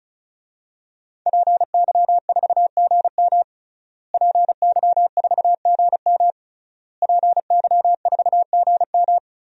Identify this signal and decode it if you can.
Morse code, heard as a keyed tone, about 34 wpm, 700 Hz. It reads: PY4GM PY4GM PY4GM
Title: DXCC-100 Call Signs - 5 Characters - Repeated 3x - 35wpm